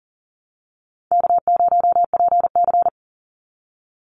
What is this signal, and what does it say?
Morse code, heard as a keyed tone, about 40 wpm, 700 Hz. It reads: K0PC
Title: Easy Call Signs Repeated 2x - 40wpm